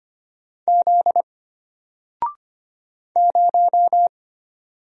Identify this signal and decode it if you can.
Morse code, heard as a keyed tone, about 25 wpm, 700 Hz. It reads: Z 0
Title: Single Letter-Number 25wpm